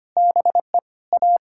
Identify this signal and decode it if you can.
Morse code, heard as a keyed tone, about 25 wpm, 700 Hz. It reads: BE A
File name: practice / 25wpm / Sentences with 3-Word Phrases - 25wpm